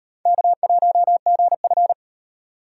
Morse code, heard as a keyed tone, about 36 wpm, 700 Hz. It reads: K1GF